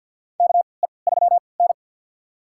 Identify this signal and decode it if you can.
Morse code, heard as a keyed tone, about 50 wpm, spaced as for 28 wpm, 700 Hz. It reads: KE3N
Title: USA Call Signs - 4 Characters - 3x Character Spacing - 50wpm